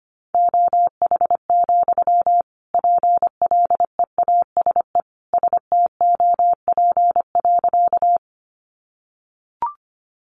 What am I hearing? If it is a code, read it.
Morse code, heard as a keyed tone, about 25 wpm, 700 Hz. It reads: OH, PLEASE STOP.
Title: Sentences from Top 1500 Words 25wpm